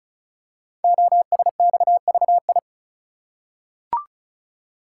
Morse code, heard as a keyed tone, about 36 wpm, 690 Hz. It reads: OSXVI
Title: Code Groups - 5 Letters - 35wpm